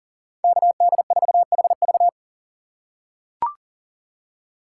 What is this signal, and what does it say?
Morse code, heard as a keyed tone, about 40 wpm, 700 Hz. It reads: KD4HV
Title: USA Call Signs - 5 Characters - 3x Character Spacing - 40wpm